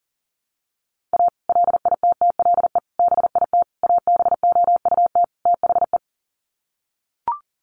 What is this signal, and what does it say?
Morse code, heard as a keyed tone, about 40 wpm, 700 Hz. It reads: A LITTLE BIT ABOUT THE
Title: Sets of 5 Words Encore2 40wpm